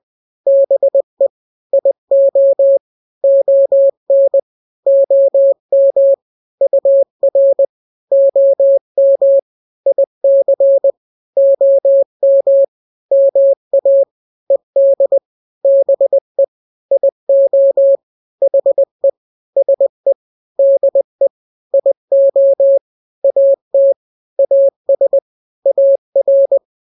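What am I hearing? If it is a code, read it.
Morse code, heard as a keyed tone, about 20 wpm, 550 Hz. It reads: BE IO ON OM UR OM IC OM MA ED BE IO HE SE DE IO AT AS AR
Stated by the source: Morse drill of two-letter n-grams at 20/18 WPM (Farnsworth), from KK7RR.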